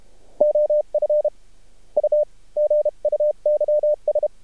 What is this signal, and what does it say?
Morse code, heard as a keyed tone, about 32 wpm, 600 Hz. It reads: OF U GUYS
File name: archived / CRSnet